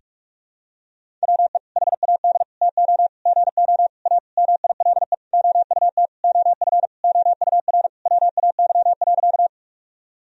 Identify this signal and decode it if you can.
Morse code, heard as a keyed tone, about 46 wpm, 700 Hz. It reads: WE HAD TO GO A MILE OUT OF OUR WAY.
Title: Sentences from Top 1000 Words - Review Part 1 - 45wpm